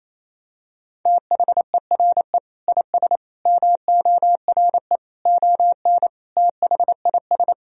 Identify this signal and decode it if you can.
Morse code, heard as a keyed tone, about 28 wpm, 700 Hz. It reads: THERE IS MORE ON THIS
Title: Sentences from Top 1000 Words - Review Part 1 - 28wpm